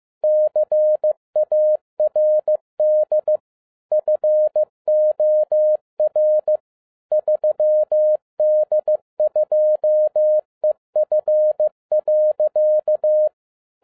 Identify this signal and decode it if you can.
Morse code, heard as a keyed tone, about 15 wpm, 610 Hz. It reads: CARD FOR 3D2EF.
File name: 2011-10-mp3 / n07-A DOT 15wpm.mp3